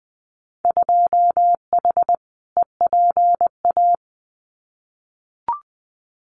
Morse code, heard as a keyed tone, about 20 wpm, 700 Hz. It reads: 2H EPA
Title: Winter Field Day Exchanges Encore 20wpm